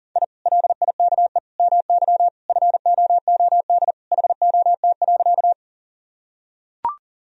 Morse code, heard as a keyed tone, about 40 wpm, 700 Hz. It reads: I LIKE MY FOOD HOT.
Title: Sentences from Top 1000 Words - Review Part 3 - 40wpm